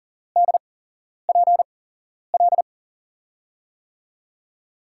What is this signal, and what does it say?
Morse code, heard as a keyed tone, about 40 wpm, spaced as for 11 wpm, 700 Hz. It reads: DPL
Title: USA Call Signs - 8x Character Spacing - 40wpm